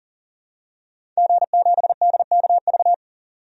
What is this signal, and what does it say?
Morse code, heard as a keyed tone, about 40 wpm, 700 Hz. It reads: G7DKV
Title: DXCC-100 Call Signs - 5 Characters - 40wpm